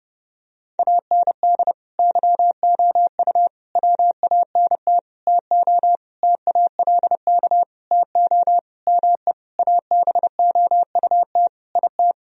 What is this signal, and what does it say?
Morse code, heard as a keyed tone, about 30 wpm, 700 Hz. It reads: AND YOU WANT TO TALK TO ME ABOUT IT
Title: Sentences from Top 1000 Words - Review Part 4 - 30wpm